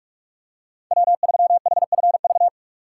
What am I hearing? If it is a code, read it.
Morse code, heard as a keyed tone, about 43 wpm, 700 Hz. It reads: W3HFV
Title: USA Call Signs - 5 Characters - 2x Character Spacing - 45wpm